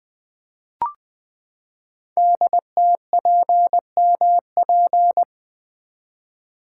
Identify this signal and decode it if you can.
Morse code, heard as a keyed tone, about 20 wpm, 700 Hz. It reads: DTPMP
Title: Code Groups - 5 Letters - 20wpm